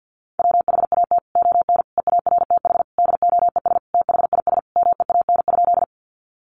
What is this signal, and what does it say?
Morse code, heard as a keyed tone, about 50 wpm, 700 Hz. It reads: WHAT ON EARTH DOES THIS MEAN?
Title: Sentences from Top 200 Words 50wpm